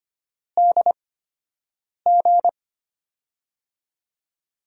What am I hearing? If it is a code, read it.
Morse code, heard as a keyed tone, about 25 wpm, spaced as for 7 wpm, 700 Hz. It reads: DG
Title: USA Call Signs - 4 Characters - 8x Character Spacing - 25wpm